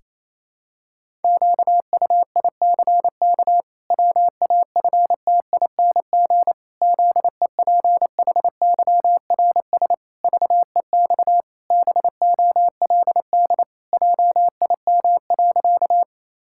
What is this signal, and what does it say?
Morse code, heard as a keyed tone, about 28 wpm, 700 Hz. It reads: QUICK WAFTING ZEPHYRS VEX BOLD JIM.
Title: Pangram Challenge v2 - 28wpm